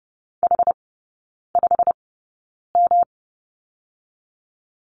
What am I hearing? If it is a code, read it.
Morse code, heard as a keyed tone, about 30 wpm, spaced as for 9 wpm, 700 Hz. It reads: H5M
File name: Lesson 19e - Letter w - Three Characters 30wpm